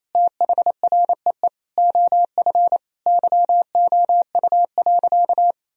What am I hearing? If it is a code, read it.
Morse code, heard as a keyed tone, about 27 wpm, 700 Hz. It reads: THREE OF YOU.
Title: Sentences from Top 1000 Words - Review Part 3 - 28wpm